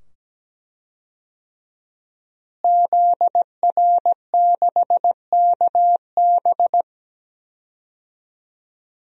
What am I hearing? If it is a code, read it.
Morse code, heard as a keyed tone, about 17 wpm, 700 Hz. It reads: ZR6KB
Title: DXCC-100 Call Signs - 5 Characters - Repeated 2x - 17wpm